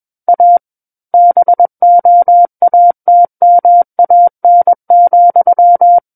Morse code, heard as a keyed tone, about 21 wpm, 700 Hz. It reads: A BOATMAN,
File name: Book One / 12-Chapter 12 - 21 WPM